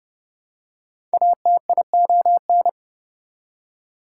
Morse code, heard as a keyed tone, about 31 wpm, 700 Hz. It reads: ATION
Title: Prefixes and Suffixes 30wpm